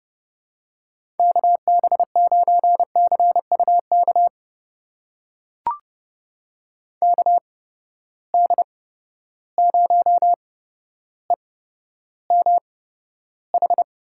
Morse code, heard as a keyed tone, about 30 wpm, spaced as for 13 wpm, 700 Hz. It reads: KB9CUK KD0EMH
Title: USA Call Signs - 6 Characters - 8x Character Spacing - 30wpm